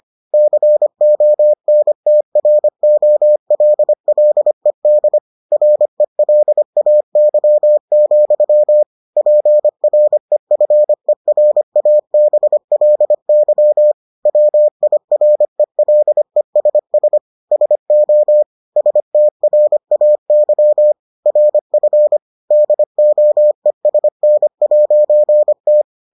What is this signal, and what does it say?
Morse code, heard as a keyed tone, about 25 wpm, 600 Hz. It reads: CONTROLLED RELAY, PREFERABLY WIRELESS SO STRAY RF DOESN'T